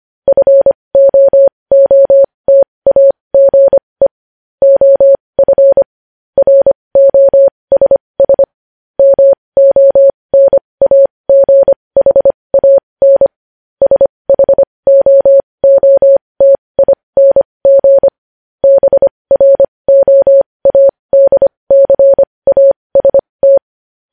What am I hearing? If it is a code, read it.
Morse code, heard as a keyed tone, about 25 wpm, spaced as for 20 wpm, 550 Hz. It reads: FOOTAGE OF ROSS MONAGHAN SHOOTING BROADCAST
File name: podcast / News headlines-1x-25x20